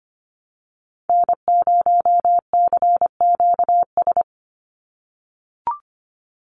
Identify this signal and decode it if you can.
Morse code, heard as a keyed tone, about 25 wpm, 700 Hz. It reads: N0CQS
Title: USA Call Signs 25wpm